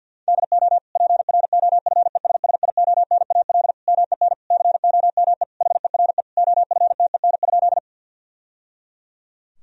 Morse code, heard as a keyed tone, about 52 wpm, 700 Hz. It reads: DO PROFESSIONAL MEN COME HERE OFTEN?
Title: Sentences from Top 1000 Words - Review Part 4 - 50wpm